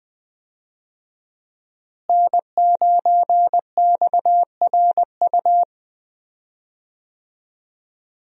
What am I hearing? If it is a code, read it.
Morse code, heard as a keyed tone, about 20 wpm, 700 Hz. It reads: N9XRU
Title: USA Call Signs - 5 Characters - Repeated 3x - 20wpm